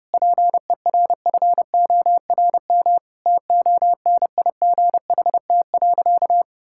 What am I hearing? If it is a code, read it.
Morse code, heard as a keyed tone, about 30 wpm, 700 Hz. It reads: PERFORM TONIGHT.
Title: Sentences from Top 1000 Words - Review Part 5 - 30wpm